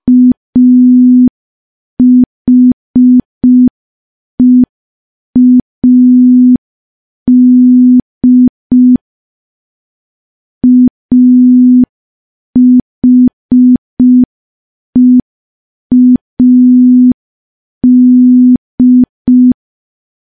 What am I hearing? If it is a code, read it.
Morse code, heard as a keyed tone, about 5 wpm, 250 Hz. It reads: AHEAD AHEAD
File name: morse / morse-training-05WPM